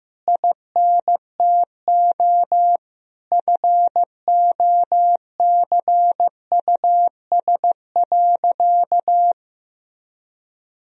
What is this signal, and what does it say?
Morse code, heard as a keyed tone, about 16 wpm, 700 Hz. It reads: INTO FOCUS.
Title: Sentences from Top 1000 Words - Review Part 4 - 15wpm